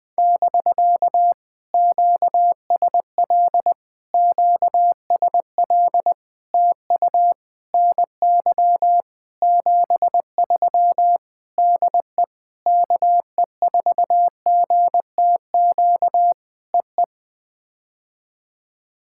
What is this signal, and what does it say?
Morse code, heard as a keyed tone, about 20 wpm, 700 Hz. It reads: <BK> QSL QSL TU NY 73 DE KE4GTQ EE
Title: POTA QSO Practice - 20wpm